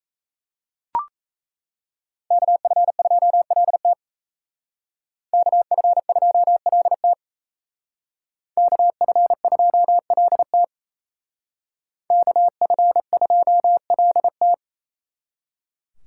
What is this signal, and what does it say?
Morse code, heard as a keyed tone, about 33 wpm, 700 Hz. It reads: KF2LT KF2LT KF2LT KF2LT
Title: Speed Racing - USA 2x2 Call Signs 28wpm